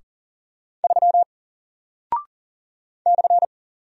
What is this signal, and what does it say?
Morse code, heard as a keyed tone, about 40 wpm, 700 Hz. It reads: 3 /
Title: Single Character (Letter:Number:Punctuation:Prosign) Alt 40wpm